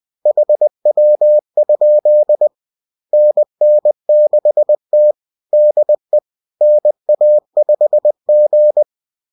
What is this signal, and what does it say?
Morse code, heard as a keyed tone, about 20 wpm, 600 Hz. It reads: HW? NN6T DE NA5G